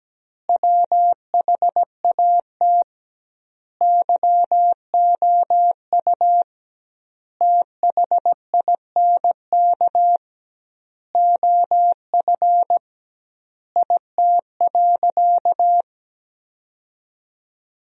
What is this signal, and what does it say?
Morse code, heard as a keyed tone, about 17 wpm, 700 Hz. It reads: WHAT YOU THINK OF IT.
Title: 2x Word-Spacing - Sentences from Top 200 Words - 17wpm